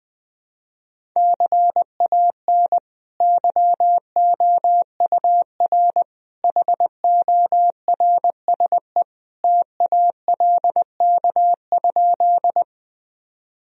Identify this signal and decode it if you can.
Morse code, heard as a keyed tone, about 20 wpm, 700 Hz. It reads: CAN YOUR HORSE TALK?
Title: Sentences from Top 1000 Words - Review Part 1 - 20wpm